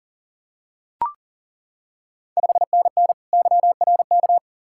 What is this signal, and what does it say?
Morse code, heard as a keyed tone, about 40 wpm, 690 Hz. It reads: 5NN YRK
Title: Ontario QSO Party Exchange 40wpm